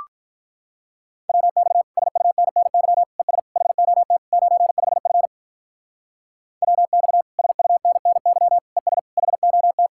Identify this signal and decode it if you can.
Morse code, heard as a keyed tone, about 44 wpm, 700 Hz. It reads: WX SUNNY ES HOT 95F WX SUNNY ES HOT
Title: Speed-Racing Common QSO Phrases - 35wpm